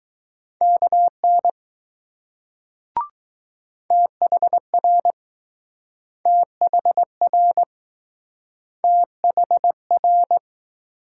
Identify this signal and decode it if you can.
Morse code, heard as a keyed tone, about 20 wpm, 700 Hz. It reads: KN THR THR THR